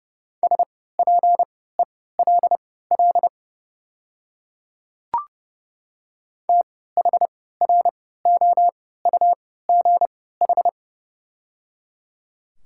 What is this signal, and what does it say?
Morse code, heard as a keyed tone, about 29 wpm, spaced as for 17 wpm, 690 Hz. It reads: SPELL THROUGH
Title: 3x Character Spacing - Top 300 Words - 30wpm